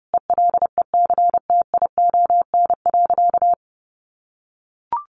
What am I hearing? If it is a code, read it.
Morse code, heard as a keyed tone, about 30 wpm, 700 Hz. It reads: ELECTION.
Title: Sentences from Top 1500 Words 30wpm